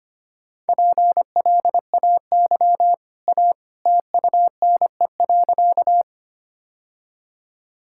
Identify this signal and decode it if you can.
Morse code, heard as a keyed tone, about 25 wpm, 700 Hz. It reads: PLAY A TUNE.
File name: Easy Sentences with Hints - 25wpm